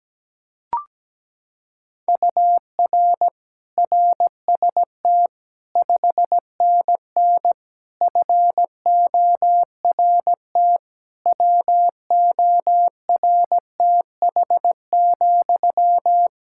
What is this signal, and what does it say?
Morse code, heard as a keyed tone, about 17 wpm, 700 Hz. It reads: UR RST 5NN FORT WORTH,
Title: Practice QSO - Exchange 1 - Beginning - 17wpm